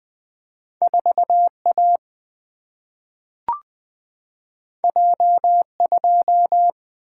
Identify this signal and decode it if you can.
Morse code, heard as a keyed tone, about 20 wpm, 700 Hz. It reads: 4 A J 2
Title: Single-Letter Single-Number 20wpm